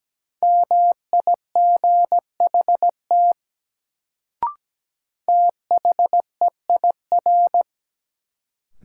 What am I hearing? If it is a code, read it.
Morse code, heard as a keyed tone, about 17 wpm, 700 Hz. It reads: MIGHT THEIR
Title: Sets of 2 Words - New Word Drill - 17wpm